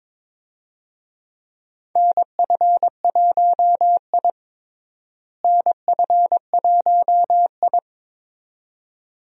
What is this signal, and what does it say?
Morse code, heard as a keyed tone, about 22 wpm, 700 Hz. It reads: NF1I NF1I
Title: USA Call Signs - 4 Characters - Repeated 2x - 22wpm